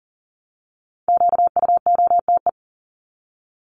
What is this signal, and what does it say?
Morse code, heard as a keyed tone, about 40 wpm, 700 Hz. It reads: QUOTE
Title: Top 1751-2000 Words 40wpm